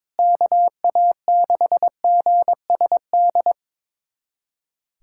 Morse code, heard as a keyed tone, about 22 wpm, 700 Hz. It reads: KA6GSD
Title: USA Callsign DE Callsign - 6 Characters - 22wpm